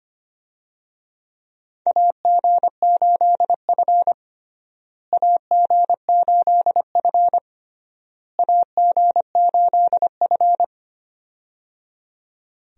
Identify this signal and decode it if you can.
Morse code, heard as a keyed tone, about 25 wpm, 700 Hz. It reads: AG8F AG8F AG8F
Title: USA Call Signs - 4 Characters - Repeated 3x - 25wpm